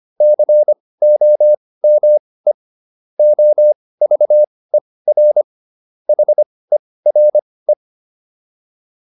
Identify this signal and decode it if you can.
Morse code, heard as a keyed tone, about 25 wpm, spaced as for 18 wpm, 600 Hz. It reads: COME OVER HERE